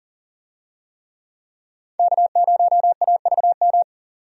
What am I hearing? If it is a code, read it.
Morse code, heard as a keyed tone, about 40 wpm, 700 Hz. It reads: K0AVM
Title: USA Call Signs - 5 Characters - 40wpm